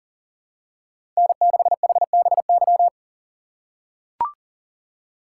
Morse code, heard as a keyed tone, about 40 wpm, 700 Hz. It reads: N6HBY
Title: USA Call Signs - 5 Characters - 6x Character Spacing - 40wpm